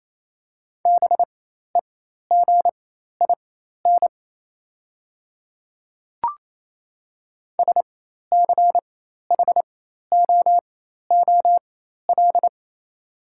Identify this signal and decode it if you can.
Morse code, heard as a keyed tone, about 28 wpm, spaced as for 13 wpm, 690 Hz. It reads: BEGIN SCHOOL
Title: 4x Character Spacing - Top 300 Words - 28wpm